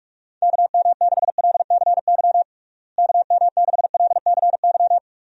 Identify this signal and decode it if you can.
Morse code, heard as a keyed tone, about 42 wpm, 700 Hz. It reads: KM6LCY KM6LCY
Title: USA Call Signs - 6 Characters - Repeated 2x - 45wpm